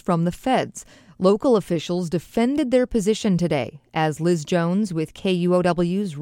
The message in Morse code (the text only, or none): none